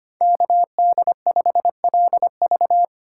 Morse code, heard as a keyed tone, about 25 wpm, 700 Hz. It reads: KD5LV
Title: USA Call Signs - 5 Characters - 4x Character Spacing - 25wpm